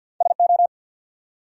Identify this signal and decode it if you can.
Morse code, heard as a keyed tone, about 53 wpm, 700 Hz. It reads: SO
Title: Top 100 Words 50wpm